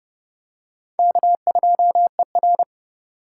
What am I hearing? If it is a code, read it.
Morse code, heard as a keyed tone, about 30 wpm, 700 Hz. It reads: K2ER